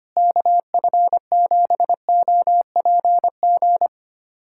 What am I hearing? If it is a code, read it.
Morse code, heard as a keyed tone, about 25 wpm, 700 Hz. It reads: KF7OPG
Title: USA Callsign DE Callsign - 6 Characters - 25wpm